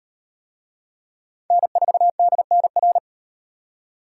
Morse code, heard as a keyed tone, about 38 wpm, 700 Hz. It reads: N4DNR